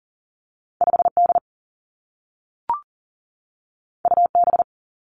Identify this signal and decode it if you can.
Morse code, heard as a keyed tone, about 41 wpm, 700 Hz. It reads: 5 D U B
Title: Two Challenging Letter-Number Combinations 40wpm